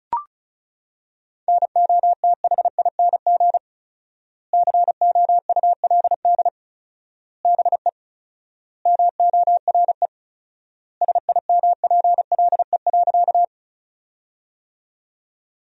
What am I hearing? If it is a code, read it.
Morse code, heard as a keyed tone, about 34 wpm, 700 Hz. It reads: NOTHING COULD BE MORE SIMPLE.
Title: 4x Word-Spacing - Sentences from Top 500 Words - 35wpm